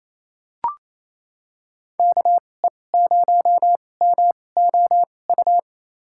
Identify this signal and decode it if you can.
Morse code, heard as a keyed tone, about 28 wpm, spaced as for 20 wpm, 700 Hz. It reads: KE0MOU
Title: USA Call Signs - 6 Characters - 2x Character Spacing - 28wpm